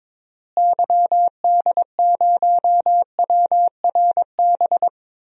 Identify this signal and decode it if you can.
Morse code, heard as a keyed tone, about 22 wpm, 700 Hz. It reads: YD0WRB
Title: DXCC-100 Call Signs - 6 Characters - Repeated 3x - 22wpm